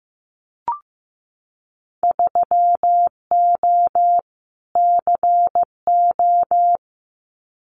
Morse code, heard as a keyed tone, about 15 wpm, 700 Hz. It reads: 3O CO
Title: Winter Field Day Exchanges Encore 15wpm